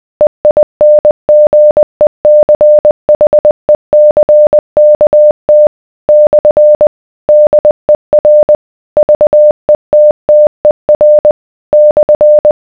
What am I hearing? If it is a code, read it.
Morse code, heard as a keyed tone, about 20 wpm, 600 Hz. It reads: EINGECHECKT / DER VETTER /